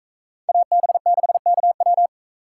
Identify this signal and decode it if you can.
Morse code, heard as a keyed tone, about 44 wpm, 700 Hz. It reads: AB6KW